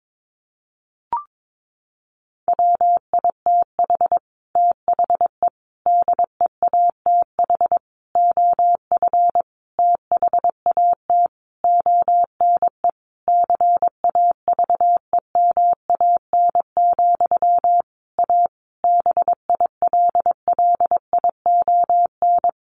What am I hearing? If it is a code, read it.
Morse code, heard as a keyed tone, about 22 wpm, 700 Hz. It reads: WITH THE DEATH OF THAT ONE CAVEMAN, A BILLION
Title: A Sound Of Thunder 22wpm